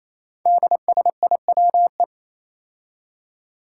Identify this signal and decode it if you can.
Morse code, heard as a keyed tone, about 27 wpm, 700 Hz. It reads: DSIWE
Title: Five Challenging Letter-Number Combinations 28wpm